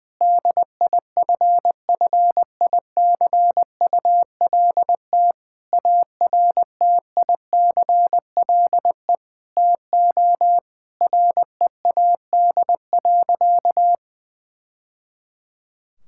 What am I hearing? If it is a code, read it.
Morse code, heard as a keyed tone, about 20 wpm, 700 Hz. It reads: DIFFICULT ARTICLE TO READ.
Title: Sentences from Top 1000 Words - Review Part 5 - 20wpm